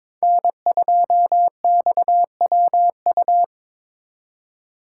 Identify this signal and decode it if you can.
Morse code, heard as a keyed tone, about 26 wpm, 700 Hz. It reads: N2XWU
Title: Speed Racing - USA 1x3 Call Signs 22wpm